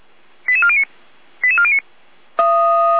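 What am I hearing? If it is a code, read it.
Five-tone selective call (ZVEI), heard as quick runs of five tones, tones of about 70 ms. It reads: 88409 88409
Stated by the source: ZVEI five-tone selective call, address 88291, received by SDR.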